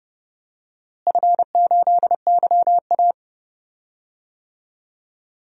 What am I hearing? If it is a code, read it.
Morse code, heard as a keyed tone, about 30 wpm, 700 Hz. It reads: F8YA
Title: DXCC-100 Call Signs - 4 Characters - Repeated 2x - 30wpm